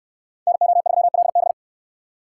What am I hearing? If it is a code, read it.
Morse code, heard as a keyed tone, about 68 wpm, 700 Hz. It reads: NO3LB